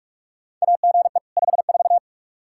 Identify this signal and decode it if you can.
Morse code, heard as a keyed tone, about 46 wpm, 700 Hz. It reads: AGE 54
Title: Practice QSO - Exchange 3 - More Info - 45wpm